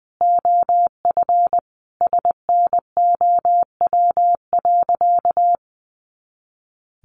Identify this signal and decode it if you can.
Morse code, heard as a keyed tone, about 20 wpm, 700 Hz. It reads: OF SNOW.
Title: Easy Phrases 20wpm